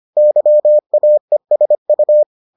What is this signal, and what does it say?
Morse code, heard as a keyed tone, about 25 wpm, 600 Hz. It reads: YAESU